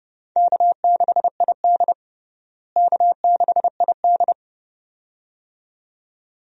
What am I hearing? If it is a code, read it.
Morse code, heard as a keyed tone, about 30 wpm, 700 Hz. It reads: K6ID K6ID
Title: USA Call Signs - 4 Characters - Repeated 2x - 30wpm